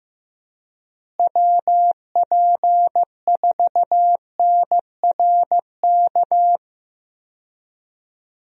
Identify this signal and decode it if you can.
Morse code, heard as a keyed tone, about 15 wpm, 700 Hz. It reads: WP4NRK
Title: USA Call Signs - 6 Characters - Repeated 3x - 15wpm